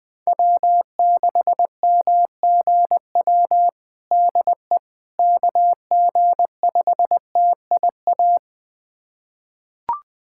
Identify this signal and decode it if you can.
Morse code, heard as a keyed tone, about 21 wpm, 700 Hz. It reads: W6MGW DE KG5TIA
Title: USA Callsign DE Callsign 20wpm